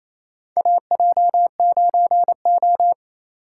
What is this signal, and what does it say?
Morse code, heard as a keyed tone, about 28 wpm, 700 Hz. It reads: AJ9O
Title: USA Call Signs - 4 Characters - 2x Character Spacing - 28wpm